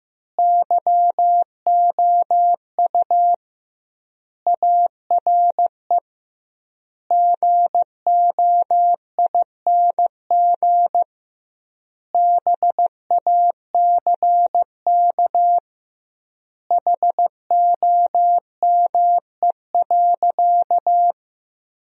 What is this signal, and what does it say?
Morse code, heard as a keyed tone, about 15 wpm, 700 Hz. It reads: YOU ARE GOING BACK HOME.
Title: 2x Word-Spacing - Sentences from Top 500 Words - 15wpm